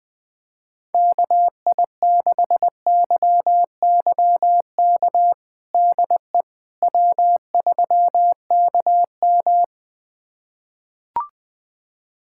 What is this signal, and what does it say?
Morse code, heard as a keyed tone, about 20 wpm, 700 Hz. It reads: KI6YYK DE W3KM
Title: USA Callsign DE Callsign 20wpm